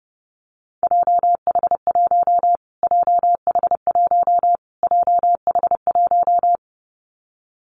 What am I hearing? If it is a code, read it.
Morse code, heard as a keyed tone, about 30 wpm, 700 Hz. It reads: JH1 JH1 JH1
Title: Callsign Prefixes for Top 25 DXCC - Repeated 3x - 30wpm